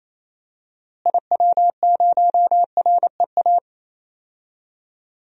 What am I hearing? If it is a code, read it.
Morse code, heard as a keyed tone, about 27 wpm, 700 Hz. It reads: IW0REA
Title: DXCC-100 Call Signs - 6 Characters - Repeated 2x - 28wpm